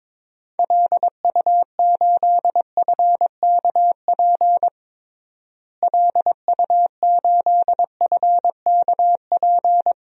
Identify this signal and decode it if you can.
Morse code, heard as a keyed tone, about 22 wpm, 700 Hz. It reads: LU8FKP LU8FKP
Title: DXCC-100 Call Signs - 6 Characters - Repeated 3x - 22wpm